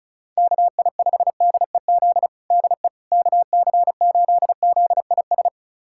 Morse code, heard as a keyed tone, about 36 wpm, 700 Hz. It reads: KI5DEZ DE KC8ZIS
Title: USA Callsign DE Callsign 35wpm